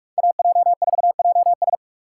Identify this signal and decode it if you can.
Morse code, heard as a keyed tone, about 42 wpm, 700 Hz. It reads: AJ4JS
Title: USA Call Signs - 5 Characters - 3x Character Spacing - 45wpm